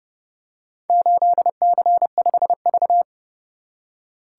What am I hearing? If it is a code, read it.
Morse code, heard as a keyed tone, about 29 wpm, 700 Hz. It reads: 8C5V